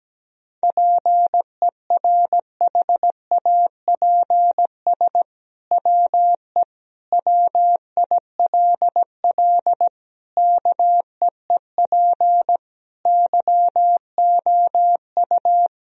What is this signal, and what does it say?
Morse code, heard as a keyed tone, about 17 wpm, 700 Hz. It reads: PERHAPS WE WILL KEEP YOU
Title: Sentences from Top 1000 Words - Review Part 3 - 17wpm